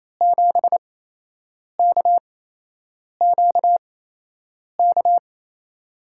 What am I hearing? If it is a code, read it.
Morse code, heard as a keyed tone, about 28 wpm, spaced as for 8 wpm, 700 Hz. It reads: 7KQK
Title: USA Call Signs - 8x Character Spacing - 28wpm